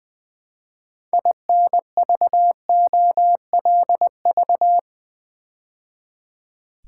Morse code, heard as a keyed tone, about 20 wpm, 700 Hz. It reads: INVOLV
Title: Build A Word - 3 Letters At A Time - 8-Letter Words - 20wpm